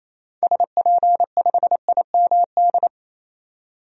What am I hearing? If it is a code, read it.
Morse code, heard as a keyed tone, about 28 wpm, 700 Hz. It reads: SP5IMD
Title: DXCC-100 Call Signs - 6 Characters - 28wpm